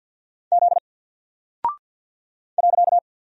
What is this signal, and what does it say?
Morse code, heard as a keyed tone, about 48 wpm, 700 Hz. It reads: Z .
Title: Single Character (Letter:Number:Punctuation:Prosign) Alt 50wpm